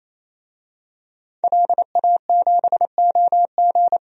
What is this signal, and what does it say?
Morse code, heard as a keyed tone, about 28 wpm, 700 Hz. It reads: LA7OG
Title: DXCC-100 Call Signs - 5 Characters - Repeated 3x - 28wpm